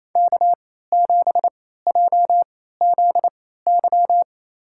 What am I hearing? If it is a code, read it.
Morse code, heard as a keyed tone, about 28 wpm, spaced as for 16 wpm, 700 Hz. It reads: K7JZY
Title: USA Call Signs - 3x Character Spacing - 28wpm